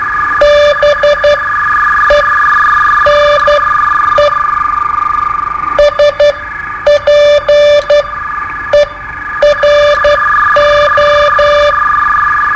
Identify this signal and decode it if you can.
Morse code, heard as a keyed tone, about 11 wpm, spaced as for 8 wpm, 590 Hz, with noise. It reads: BENE SPERO
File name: qso-01